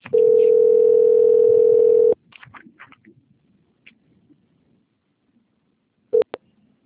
Ringback tone (North America), 440 + 480 Hz.